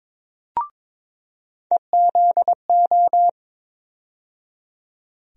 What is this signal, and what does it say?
Morse code, heard as a keyed tone, about 22 wpm, 700 Hz. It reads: EZO
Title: USA Call Sign Postfixes - 3 Letters 22wpm